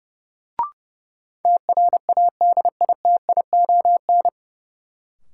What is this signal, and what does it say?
Morse code, heard as a keyed tone, about 32 wpm, 700 Hz. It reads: TRADITION
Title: Sentences from Top 1500 Words - New Word Drill Alt - 30wpm